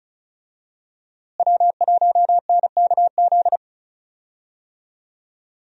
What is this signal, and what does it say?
Morse code, heard as a keyed tone, about 36 wpm, 700 Hz. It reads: W1NKZ
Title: USA Call Signs - 5 Characters - Repeated 3x - 35wpm